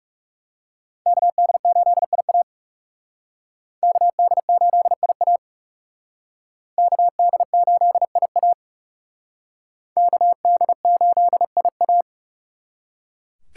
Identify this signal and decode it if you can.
Morse code, heard as a keyed tone, about 37 wpm, 700 Hz. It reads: KD8IA KD8IA KD8IA KD8IA
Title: Speed Racing - USA 2x2 Call Signs 30wpm